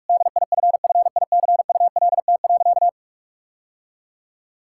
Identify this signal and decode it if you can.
Morse code, heard as a keyed tone, about 46 wpm, 700 Hz. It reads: DIFFICULT.
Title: Sentences from Top 1000 Words - Review Part 5 - 45wpm